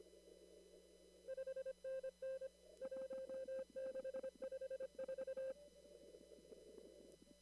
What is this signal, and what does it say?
Morse code, heard as a keyed tone, about 25 wpm, 530 Hz. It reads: 5NN 1654
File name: CW / kt5j wpx cw-op-k5pi